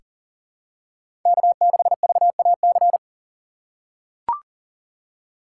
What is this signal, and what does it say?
Morse code, heard as a keyed tone, about 38 wpm, 700 Hz. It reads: K6VAC
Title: USA Call Signs - 8x Character Spacing - 40wpm